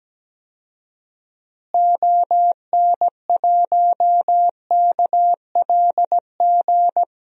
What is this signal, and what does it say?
Morse code, heard as a keyed tone, about 17 wpm, 700 Hz. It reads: ON1KLG
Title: DXCC-100 Call Signs - 6 Characters - Repeated 3x - 17wpm